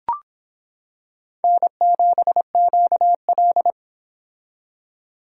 Morse code, heard as a keyed tone, about 25 wpm, 700 Hz. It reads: N7QL